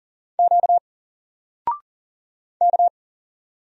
Morse code, heard as a keyed tone, about 40 wpm, 700 Hz. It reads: Q K
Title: Single Character (Letter:Number:Punctuation:Prosign) 40wpm